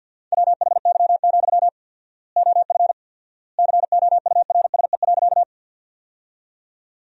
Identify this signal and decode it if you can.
Morse code, heard as a keyed tone, about 52 wpm, 700 Hz. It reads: WHY, OF COURSE.
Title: 4x Word-Spacing - Sentences from Top 500 Words - 50wpm